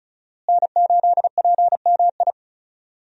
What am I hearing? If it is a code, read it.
Morse code, heard as a keyed tone, about 36 wpm, 700 Hz. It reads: N8PMI
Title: USA Call Signs - 5 Characters - Repeated 3x - 35wpm